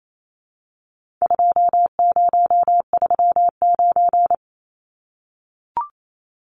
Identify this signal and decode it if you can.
Morse code, heard as a keyed tone, about 28 wpm, 700 Hz. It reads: 2039
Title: Numbers - 4-Digit 28wpm